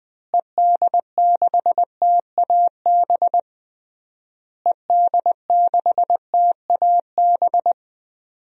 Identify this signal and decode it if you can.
Morse code, heard as a keyed tone, about 20 wpm, 700 Hz. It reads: ED6TAB ED6TAB
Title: DXCC-100 Call Signs - 6 Characters - Repeated 3x - 20wpm